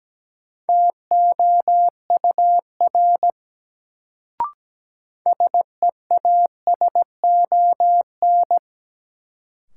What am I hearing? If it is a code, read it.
Morse code, heard as a keyed tone, about 17 wpm, 700 Hz. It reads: TOUR SEASON
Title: Sentences from Top 1400 Words - New Word Drill - 17wpm